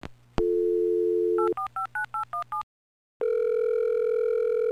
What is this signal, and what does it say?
DTMF, heard as telephone touch tones, tones of about 120 ms. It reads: 8459817